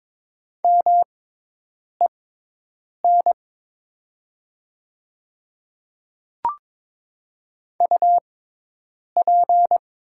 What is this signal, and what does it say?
Morse code, heard as a keyed tone, about 22 wpm, spaced as for 8 wpm, 700 Hz. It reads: MEN UP
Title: 6x Character Spacing - Top 300 Words - 22wpm